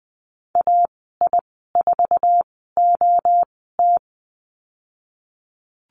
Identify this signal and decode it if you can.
Morse code, heard as a keyed tone, about 20 wpm, spaced as for 15 wpm, 700 Hz. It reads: AI4OT
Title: 055 Lesson 10g - Letter r - Call Signs 20wpm